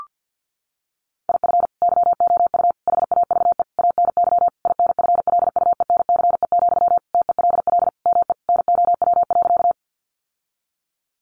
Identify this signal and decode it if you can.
Morse code, heard as a keyed tone, about 50 wpm, 700 Hz. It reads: IF YOU HAVE ANY INFLUENCE, TELL ME NOW.
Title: Sentences from Top 1100 Words 50wpm